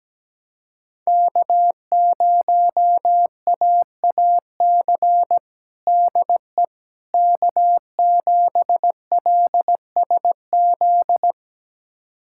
Morse code, heard as a keyed tone, about 17 wpm, 700 Hz. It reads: K0AAC DE K7LSZ
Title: USA Callsign DE Callsign - 5 Characters - 17wpm